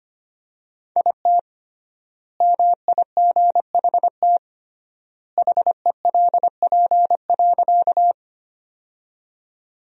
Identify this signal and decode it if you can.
Morse code, heard as a keyed tone, about 25 wpm, 700 Hz. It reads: IT MIGHT HELP.
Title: 3x Word-Spacing - Sentences from Top 300 Words - 25wpm